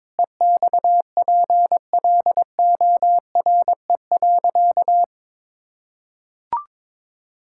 Morse code, heard as a keyed tone, about 22 wpm, 700 Hz. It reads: EXPLORE.
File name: practice / 22wpm / Sentences with 2-Word Phrases - 22wpm